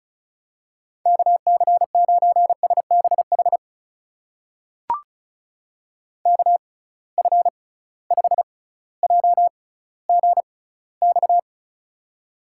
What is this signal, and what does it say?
Morse code, heard as a keyed tone, about 34 wpm, spaced as for 18 wpm, 700 Hz. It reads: KC9SBH KF5JGX
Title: USA Call Signs - 6 Characters - 6x Character Spacing - 35wpm